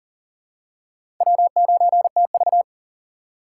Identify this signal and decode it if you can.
Morse code, heard as a keyed tone, about 37 wpm, 700 Hz. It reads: W9TV
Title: USA Call Signs - 4 Characters - 3x Character Spacing - 40wpm